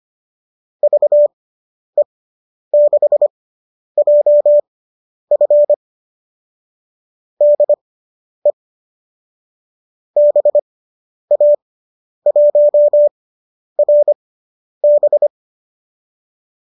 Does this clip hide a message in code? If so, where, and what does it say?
Yes: Morse code from 0.8 to 15.3 s, heard as a keyed tone, about 25 wpm, spaced as for 10 wpm, 570 Hz. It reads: VE6JF DE BA1RB